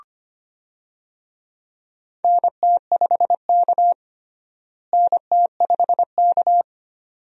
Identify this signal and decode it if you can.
Morse code, heard as a keyed tone, about 25 wpm, 700 Hz. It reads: NT5K NT5K
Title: USA Call Signs - 4 Characters - Repeated 2x - 25wpm